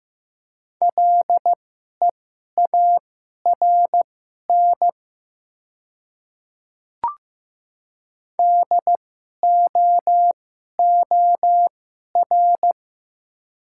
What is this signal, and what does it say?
Morse code, heard as a keyed tone, about 15 wpm, spaced as for 11 wpm, 700 Hz. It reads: LEARN DOOR
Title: 2x Character Spacing - Top 300 Words - 15wpm